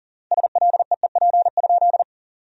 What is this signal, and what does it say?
Morse code, heard as a keyed tone, about 40 wpm, 700 Hz. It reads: SLEEP?
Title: 3x Word-Spacing - Sentences from Top 500 Words - 40wpm